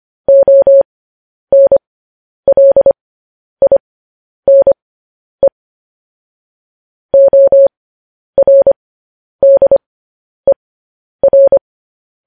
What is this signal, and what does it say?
Morse code, heard as a keyed tone, about 25 wpm, spaced as for 10 wpm, 550 Hz. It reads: ONLINE ORDER
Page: News Headlines in Morse Code at 10 WPM